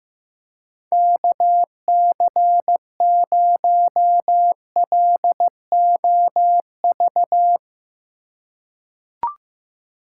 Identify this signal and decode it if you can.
Morse code, heard as a keyed tone, about 15 wpm, 700 Hz. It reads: KC0LOV
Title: USA Call Signs - 6 Characters - 2x Character Spacing - 15wpm